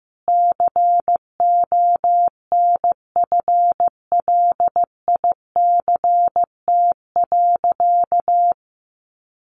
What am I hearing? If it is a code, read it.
Morse code, heard as a keyed tone, about 15 wpm, 700 Hz. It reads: CONFLICT.
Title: Sentences from Top 1300 Words 15wpm